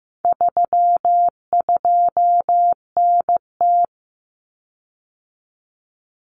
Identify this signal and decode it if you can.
Morse code, heard as a keyed tone, about 15 wpm, 700 Hz. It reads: 32NT
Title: Common Cut Numbers - 4-Digit 15wpm